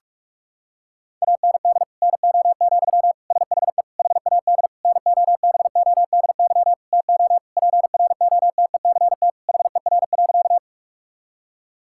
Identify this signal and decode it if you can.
Morse code, heard as a keyed tone, about 46 wpm, 700 Hz. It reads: AND NO, SHE HAD NOBODY TO PROTECT HER.
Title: Sentences from Top 1600 Words 45wpm